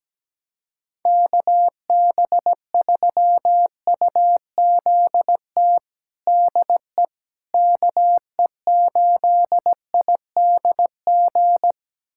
Morse code, heard as a keyed tone, about 17 wpm, 700 Hz. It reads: KB3UZT DE KE8IDG
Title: USA Callsign DE Callsign 17wpm